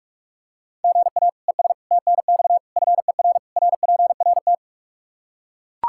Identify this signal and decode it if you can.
Morse code, heard as a keyed tone, about 42 wpm, 700 Hz. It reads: GA ES TNX FER RPRT
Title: Practice QSO - Exchange 1 - Beginning - 45wpm